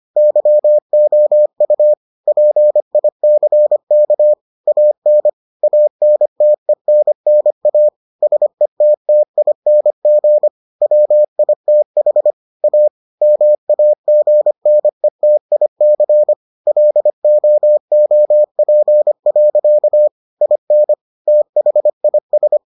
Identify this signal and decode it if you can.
Morse code, heard as a keyed tone, about 25 wpm, 600 Hz. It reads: YOU PICK AN ANTENNA SETTING WITH A MAGNETIC LOOP. IN THIS